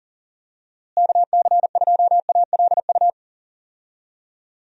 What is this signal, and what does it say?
Morse code, heard as a keyed tone, about 40 wpm, 700 Hz. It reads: KC2ALU
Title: USA Call Signs - 6 Characters - Repeated 2x - 40wpm